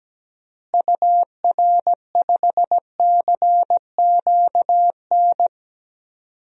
Morse code, heard as a keyed tone, about 17 wpm, 700 Hz. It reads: UR5CQN
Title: DXCC-100 Call Signs - 6 Characters - Repeated 3x - 17wpm